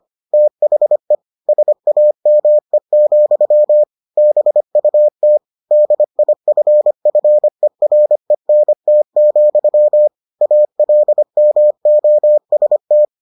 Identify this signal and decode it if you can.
Morse code, heard as a keyed tone, about 25 wpm, 600 Hz. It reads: THE SAME, BUT DIFFERENT, ALMOST